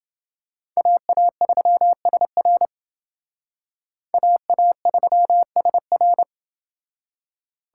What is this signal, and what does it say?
Morse code, heard as a keyed tone, about 28 wpm, 700 Hz. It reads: AA3SR AA3SR